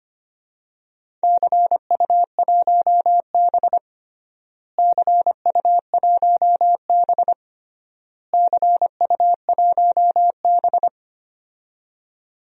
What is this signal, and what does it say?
Morse code, heard as a keyed tone, about 25 wpm, 700 Hz. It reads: CU1B CU1B CU1B
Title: DXCC-100 Call Signs - 4 Characters - Repeated 3x - 25wpm